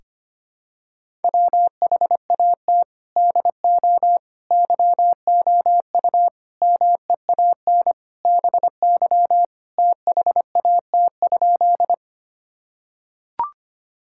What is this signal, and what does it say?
Morse code, heard as a keyed tone, about 25 wpm, 700 Hz. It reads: WHAT DO YOU MEAN BY THAT?
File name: practice / 25wpm / Sentences with 4-Word Phrases - 25wpm